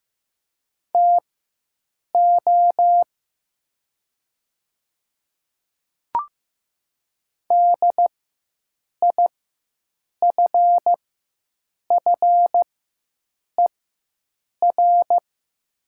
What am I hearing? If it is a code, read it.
Morse code, heard as a keyed tone, about 15 wpm, spaced as for 7 wpm, 700 Hz. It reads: TO DIFFER
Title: 4x Character Spacing - Top 500 Words - 15wpm